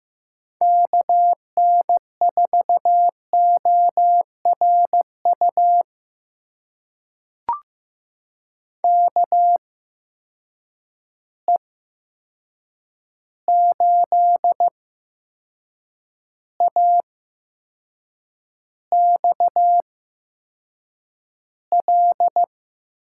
Morse code, heard as a keyed tone, about 15 wpm, spaced as for 6 wpm, 700 Hz. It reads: KN4ORU KE8AXL
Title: USA Call Signs - 6 Characters - 8x Character Spacing - 15wpm